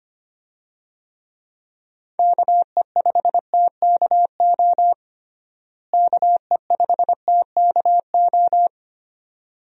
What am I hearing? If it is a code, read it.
Morse code, heard as a keyed tone, about 25 wpm, 700 Hz. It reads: KE5TKO KE5TKO
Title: USA Call Signs - 6 Characters - Repeated 2x - 25wpm